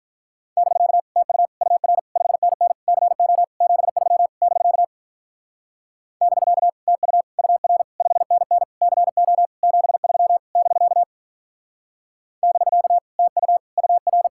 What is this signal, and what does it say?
Morse code, heard as a keyed tone, about 43 wpm, 700 Hz. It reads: <BK> TU UR 5NN CO 73 <BK> <BK> TU UR 5NN CO 73 <BK> <BK> TU UR
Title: Speed-Racing Common QSO Phrases Encore - 35wpm